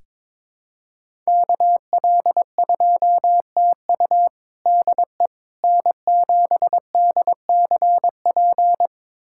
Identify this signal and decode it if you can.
Morse code, heard as a keyed tone, about 21 wpm, 700 Hz. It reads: KL2TU DE N7DCP
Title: USA Callsign DE Callsign - 5 Characters - 22wpm